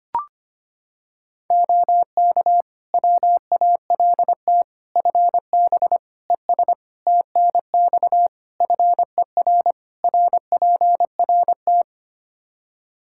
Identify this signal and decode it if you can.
Morse code, heard as a keyed tone, about 25 wpm, 700 Hz. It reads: OK WALT FB ES TNX FER RPRT
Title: Practice QSO - Exchange 2 - Info - 25wpm